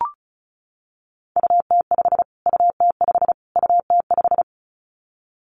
Morse code, heard as a keyed tone, about 34 wpm, 700 Hz. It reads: UT5 UT5 UT5
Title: Callsign Prefixes for Top 25 DXCC - Repeated 3x - 35wpm